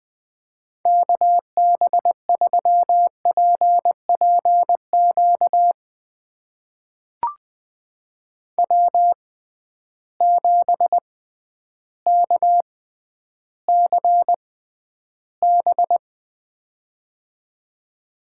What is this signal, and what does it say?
Morse code, heard as a keyed tone, about 20 wpm, spaced as for 11 wpm, 700 Hz. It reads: KB3PPQ W7KCB
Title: USA Call Signs - 6x Character Spacing - 20wpm